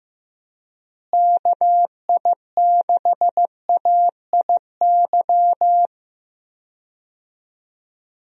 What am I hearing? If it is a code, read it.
Morse code, heard as a keyed tone, about 15 wpm, 700 Hz. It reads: KI6AIY
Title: USA Call Signs - 6 Characters - Repeated 3x - 15wpm